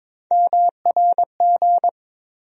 Morse code, heard as a keyed tone, about 22 wpm, 700 Hz. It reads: MRG
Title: USA Call Sign Postfixes - 3 Letters 22wpm